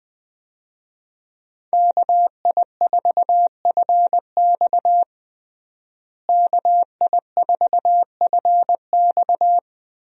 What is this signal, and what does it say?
Morse code, heard as a keyed tone, about 21 wpm, 700 Hz. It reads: KI4FX KI4FX
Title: USA Call Signs - 5 Characters - Repeated 3x - 20wpm